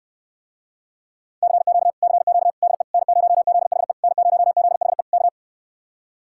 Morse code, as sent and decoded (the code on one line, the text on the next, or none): -.-. --.- / -.-. --.- / -.. / -. ----- --.. .... / -. ----- --.. .... / -.-
CQ CQ DE N0ZHE N0ZHE K